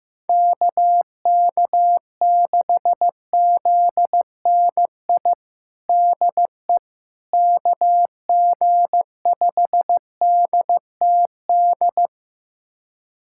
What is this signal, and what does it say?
Morse code, heard as a keyed tone, about 15 wpm, 700 Hz. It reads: KK6ZNI DE KG5DTD
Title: USA Callsign DE Callsign - 6 Characters - 15wpm